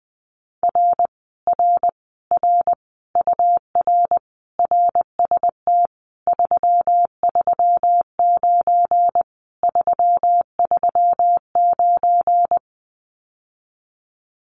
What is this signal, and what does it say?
Morse code, heard as a keyed tone, about 20 wpm, 700 Hz. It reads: R R R UR RST 339 339
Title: Common QSO Phrases Encore 20wpm